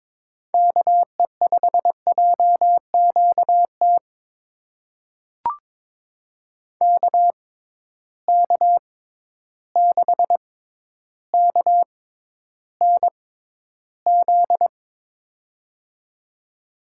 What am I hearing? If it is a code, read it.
Morse code, heard as a keyed tone, about 21 wpm, spaced as for 11 wpm, 700 Hz. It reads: KE5JQT KK6KNZ
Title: USA Call Signs - 6 Characters - 6x Character Spacing - 22wpm